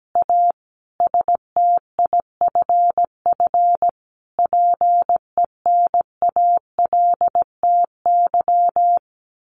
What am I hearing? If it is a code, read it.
Morse code, heard as a keyed tone, about 17 wpm, 700 Hz. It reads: A STIFF PENALTY
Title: A Sound Of Thunder 17wpm